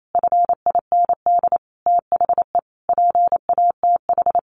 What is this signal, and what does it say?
Morse code, heard as a keyed tone, about 28 wpm, 700 Hz. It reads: FIND THE PATH
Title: Sentences from Top 1400 Words 28wpm